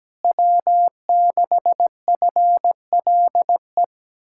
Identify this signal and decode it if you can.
Morse code, heard as a keyed tone, about 17 wpm, 700 Hz. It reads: W6FLE
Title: Speed-Racing Common QSO Phrases Encore - 17wpm